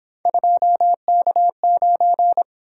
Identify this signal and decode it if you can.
Morse code, heard as a keyed tone, about 26 wpm, 700 Hz. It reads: 2K9
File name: Speed-Racing - Three Letter-Number Combinations - 22wpm